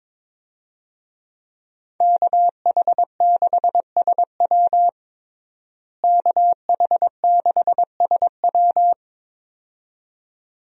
Morse code, heard as a keyed tone, about 22 wpm, 700 Hz. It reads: KH6SW KH6SW
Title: DXCC-100 Call Signs - 5 Characters - Repeated 2x - 22wpm